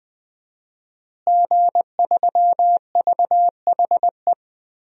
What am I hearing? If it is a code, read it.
Morse code, heard as a keyed tone, about 20 wpm, 700 Hz. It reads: G3VHE
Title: DXCC-100 Call Signs - 5 Characters - Repeated 2x - 20wpm